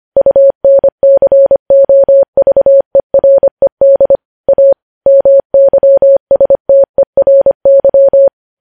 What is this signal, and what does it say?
Morse code, heard as a keyed tone, about 25 wpm, 550 Hz. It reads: UNCOVERED A MYSTERY
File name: podcast / News headlines-1x-25x25